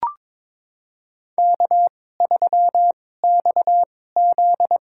Morse code, heard as a keyed tone, about 22 wpm, spaced as for 16 wpm, 700 Hz. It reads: K3XZ
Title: USA Call Signs - 4 Characters - 2x Character Spacing - 22wpm